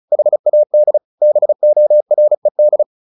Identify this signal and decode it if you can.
Morse code, heard as a keyed tone, about 34 wpm, 600 Hz. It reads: HAD BORED
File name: the old man and the sea / the old man and the sea 1 35wpm